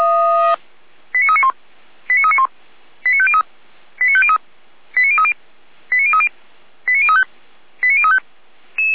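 Five-tone selective call (ZVEI), heard as quick runs of five tones, tones of about 70 ms. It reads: 89391 89391 89503 89503 89030 89030 89035 89035
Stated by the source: ZVEI five-tone selective call 89125 received by SDR.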